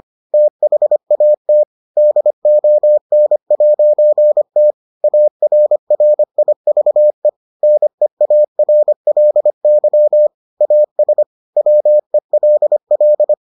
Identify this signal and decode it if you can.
Morse code, heard as a keyed tone, about 25 wpm, 600 Hz. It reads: THAT DON'T ARRIVE NEARLY AS WELL